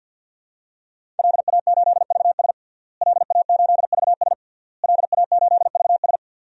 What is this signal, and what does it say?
Morse code, heard as a keyed tone, about 51 wpm, 700 Hz. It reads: LA8VS LA8VS LA8VS
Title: DXCC-100 Call Signs - 5 Characters - Repeated 3x - 50wpm